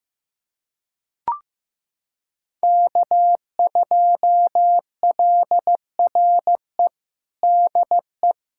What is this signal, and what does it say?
Morse code, heard as a keyed tone, about 15 wpm, 700 Hz. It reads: K2LRE DE
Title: USA Callsign DE Callsign - 5 Characters - 15wpm